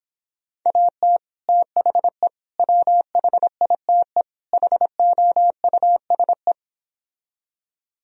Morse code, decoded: AT THE WHITE HOUSE

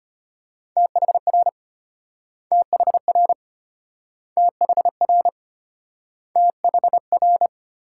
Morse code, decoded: THR THR THR THR

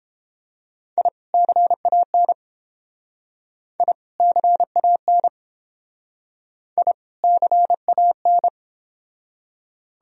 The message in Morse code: I CAN I CAN I CAN

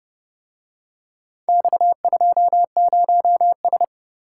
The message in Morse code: X20S

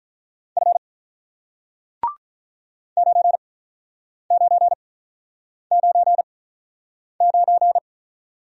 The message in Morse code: F 9999

